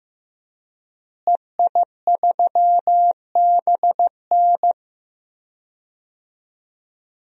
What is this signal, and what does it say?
Morse code, heard as a keyed tone, about 15 wpm, 700 Hz. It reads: EI3BN
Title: DXCC-100 Call Signs - 5 Characters - Repeated 3x - 15wpm